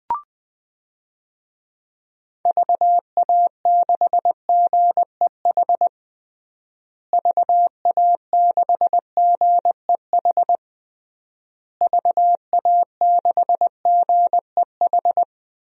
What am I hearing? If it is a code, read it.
Morse code, heard as a keyed tone, about 21 wpm, 700 Hz. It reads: VA6GEH VA6GEH VA6GEH
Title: DXCC-100 Call Signs - 6 Characters - Repeated 3x - 20wpm